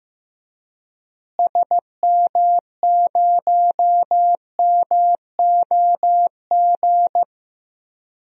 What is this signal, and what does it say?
Morse code, heard as a keyed tone, about 15 wpm, 700 Hz. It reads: SM0MOG
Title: DXCC-100 Call Signs - 6 Characters - Repeated 2x - 15wpm